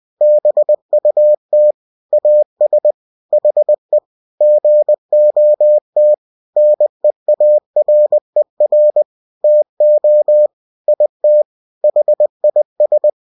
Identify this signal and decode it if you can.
Morse code, heard as a keyed tone, about 20 wpm, 600 Hz. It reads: BUT AS HE GOT NEARER TO IT HIS
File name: cw-books / winnie-the-pooh / winnie-20wpm-page-0040